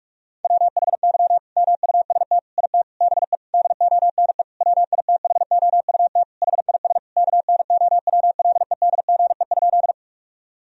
Morse code, heard as a keyed tone, about 41 wpm, 700 Hz. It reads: WHY MUST IT BE DONE WITHOUT HIS KNOWLEDGE?
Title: Sentences from Top 1000 Words - Review Part 3 - 45wpm